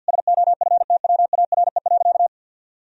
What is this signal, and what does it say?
Morse code, heard as a keyed tone, about 50 wpm, 700 Hz. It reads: SOFTWARE.
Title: Sentences from Top 1000 Words - Review Part 3 - 50wpm